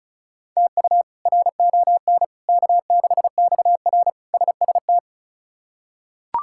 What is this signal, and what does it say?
Morse code, heard as a keyed tone, about 36 wpm, 700 Hz. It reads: TU RON K6XR SST
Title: K1USN SST QSO Practice - 35wpm